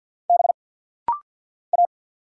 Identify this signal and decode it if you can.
Morse code, heard as a keyed tone, about 48 wpm, 700 Hz. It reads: B A
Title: Single Letters - Rapid-Fire 50wpm